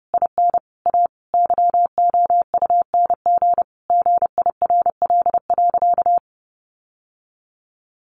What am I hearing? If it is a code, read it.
Morse code, heard as a keyed tone, about 28 wpm, 700 Hz. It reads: IN A YOUNG GIRL.
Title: Sentences from Top 1400 Words 30wpm